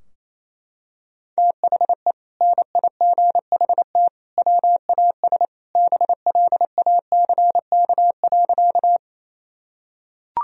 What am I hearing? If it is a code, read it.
Morse code, heard as a keyed tone, about 28 wpm, 700 Hz. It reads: THE NIGHT WAS BLACK.
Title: Sentences from Top 1000 Words - Review Part 2 - 28wpm